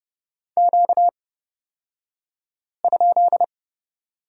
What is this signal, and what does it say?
Morse code, heard as a keyed tone, about 30 wpm, spaced as for 5 wpm, 700 Hz. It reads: Q?
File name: Lesson 38d - Letter q - Two Characters 30wpm